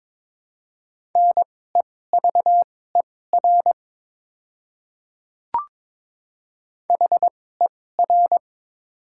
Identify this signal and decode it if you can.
Morse code, heard as a keyed tone, about 22 wpm, spaced as for 16 wpm, 700 Hz. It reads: NEVER HER
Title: 2x Character Spacing - Top 300 Words - 22wpm